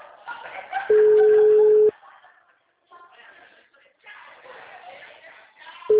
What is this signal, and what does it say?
Ringback tone (Europe, 425 Hz), with speech.